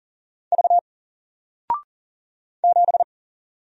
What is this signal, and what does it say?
Morse code, heard as a keyed tone, about 40 wpm, 700 Hz. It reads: V 7
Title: Single Letter-Number Alt 40wpm